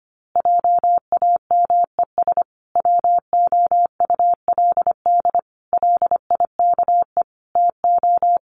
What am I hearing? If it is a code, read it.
Morse code, heard as a keyed tone, about 25 wpm, 700 Hz. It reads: JAMES WOULD LIKE TO
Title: Easy Phrases 25wpm